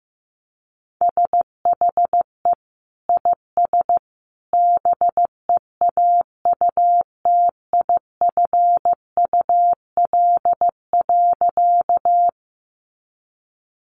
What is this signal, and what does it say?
Morse code, heard as a keyed tone, about 15 wpm, 700 Hz. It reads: SHE IS BEAUTIFUL.
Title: Sentences from Top 900 Words 15wpm